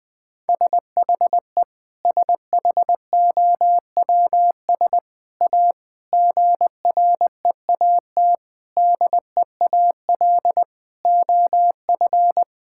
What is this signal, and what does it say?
Morse code, heard as a keyed tone, about 20 wpm, 700 Hz. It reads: SHE SHOWS A GREAT DEAL OF
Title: Sentences with 4-Word Phrases - Spoken Beginning - 20wpm